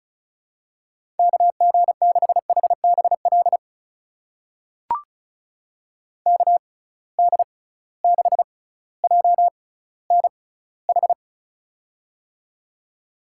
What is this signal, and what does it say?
Morse code, heard as a keyed tone, about 36 wpm, spaced as for 18 wpm, 700 Hz. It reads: KG6HBL KD6JNH
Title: USA Call Signs - 6x Character Spacing - 35wpm